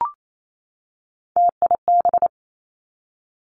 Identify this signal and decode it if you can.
Morse code, heard as a keyed tone, about 28 wpm, 700 Hz. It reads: TIB
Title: Three Challenging Letter-Number Combinations 28wpm